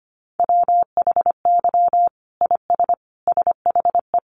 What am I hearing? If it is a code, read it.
Morse code, heard as a keyed tone, about 25 wpm, 700 Hz. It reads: WHY IS SHE
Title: Sentences from Top 900 Words 25wpm